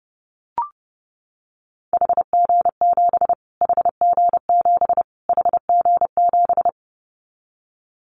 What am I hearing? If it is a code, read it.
Morse code, heard as a keyed tone, about 30 wpm, 700 Hz. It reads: HG7 HG7 HG7
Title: Callsign Prefixes for Top 25 DXCC - Repeated 3x - 30wpm